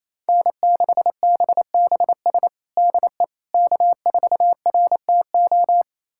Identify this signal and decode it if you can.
Morse code, heard as a keyed tone, about 28 wpm, 700 Hz. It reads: N6BBS DE K4RTO
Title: USA Callsign DE Callsign - 5 Characters - 28wpm